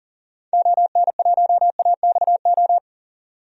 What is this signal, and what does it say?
Morse code, heard as a keyed tone, about 40 wpm, 700 Hz. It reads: ON1AXO
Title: DXCC-100 Call Signs - 6 Characters - 40wpm